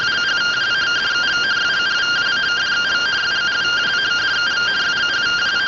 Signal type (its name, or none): two-tone teleprinter signal